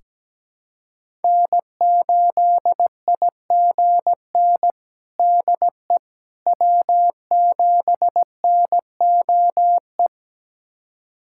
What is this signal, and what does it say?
Morse code, heard as a keyed tone, about 17 wpm, 700 Hz. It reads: N8IGN DE W7NOE
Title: USA Callsign DE Callsign - 5 Characters - 17wpm